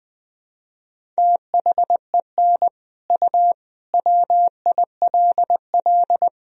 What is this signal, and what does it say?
Morse code, heard as a keyed tone, about 20 wpm, 700 Hz. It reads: THEN U WILL
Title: Sentences from Top 400 Words and Abbreviations 20wpm